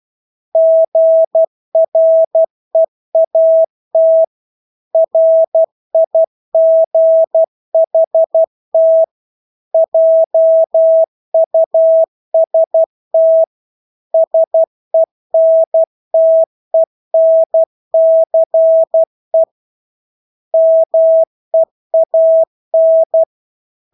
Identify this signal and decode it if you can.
Morse code, heard as a keyed tone, about 12 wpm, 650 Hz. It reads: GREAT RIGHT JUST SENTENCE MEAN
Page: Engelske ord 12 wpm